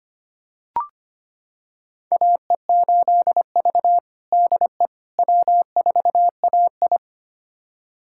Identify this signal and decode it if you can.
Morse code, heard as a keyed tone, about 25 wpm, 700 Hz. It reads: AE8V DE W4AI
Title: USA Callsign DE Callsign - 4 Characters - 25wpm